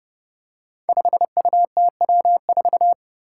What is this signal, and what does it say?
Morse code, heard as a keyed tone, about 30 wpm, 700 Hz. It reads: 5UTW4
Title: Five Challenging Letter-Number Combinations 30wpm